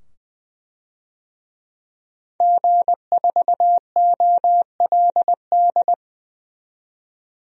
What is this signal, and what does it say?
Morse code, heard as a keyed tone, about 20 wpm, 700 Hz. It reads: G4OLD
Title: DXCC-100 Call Signs - 5 Characters - 20wpm